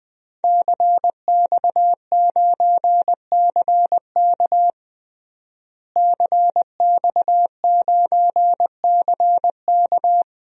Morse code, heard as a keyed tone, about 21 wpm, 700 Hz. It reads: CX9CK CX9CK
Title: DXCC-100 Call Signs - 5 Characters - Repeated 2x - 20wpm